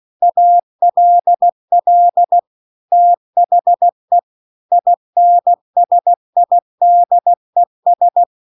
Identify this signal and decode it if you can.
Morse code, heard as a keyed tone, about 16 wpm, 700 Hz. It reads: ALL THE INSIDES
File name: CW-The-Big-Twoheaded-River-2-16WPM